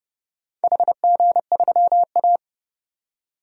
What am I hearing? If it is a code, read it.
Morse code, heard as a keyed tone, about 28 wpm, 700 Hz. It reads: HG3A
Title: DXCC-100 Call Signs - 4 Characters - 30wpm